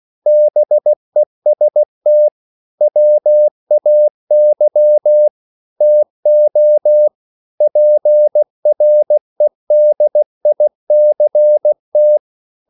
Morse code, heard as a keyed tone, about 16 wpm, 600 Hz. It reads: BEST WAY TO PREDICT
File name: CW / Practice / EnglishQuotations0416